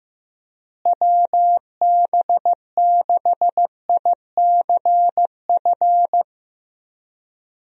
Morse code, heard as a keyed tone, about 15 wpm, 700 Hz. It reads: WB6ICF
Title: USA Call Signs - 6 Characters - Repeated 2x - 15wpm